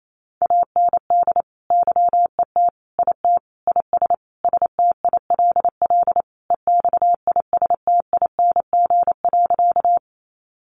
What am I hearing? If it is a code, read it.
Morse code, heard as a keyed tone, about 28 wpm, 700 Hz. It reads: AND YET IT IS STILL EXISTING.
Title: Sentences from Top 1500 Words 28wpm